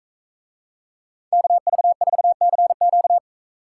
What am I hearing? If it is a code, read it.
Morse code, heard as a keyed tone, about 40 wpm, 700 Hz. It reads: KV4CQ